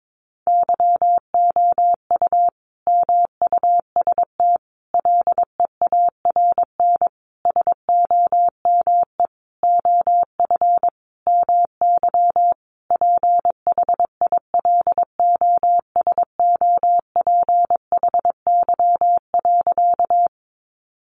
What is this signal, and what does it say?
Morse code, heard as a keyed tone, about 22 wpm, 700 Hz. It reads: YOU MUST LEARN SOME OF MY PHILOSOPHY.
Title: Sentences from Top 900 Words 22wpm